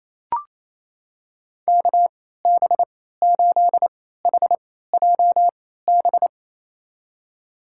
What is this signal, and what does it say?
Morse code, heard as a keyed tone, about 28 wpm, spaced as for 16 wpm, 700 Hz. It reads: KB8HJB
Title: USA Call Signs - 6 Characters - 3x Character Spacing - 28wpm